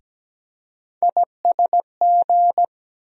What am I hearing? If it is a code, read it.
Morse code, heard as a keyed tone, about 17 wpm, 700 Hz. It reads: ISG
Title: USA Call Sign Postfixes - 3 Letters 17wpm